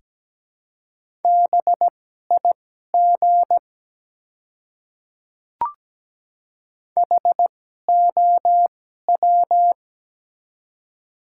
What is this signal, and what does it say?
Morse code, heard as a keyed tone, about 17 wpm, spaced as for 12 wpm, 700 Hz. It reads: BIG HOW
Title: 2x Character Spacing - Top 200 Words - 17wpm